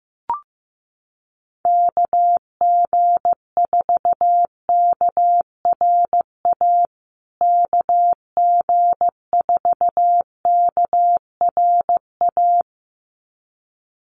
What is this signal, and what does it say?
Morse code, heard as a keyed tone, about 15 wpm, 700 Hz. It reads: KG4KRA KG4KRA
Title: USA Call Signs Repeated 2x 15wpm